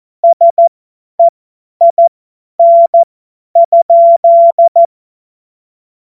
Morse code, heard as a keyed tone, about 14 wpm, spaced as for 10 wpm, 680 Hz. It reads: SEIN?
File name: Morsekurs / Witze / Witz 09. Februar 2021, 50 F70